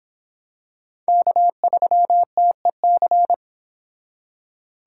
Morse code, heard as a keyed tone, about 25 wpm, 700 Hz. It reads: K3TEC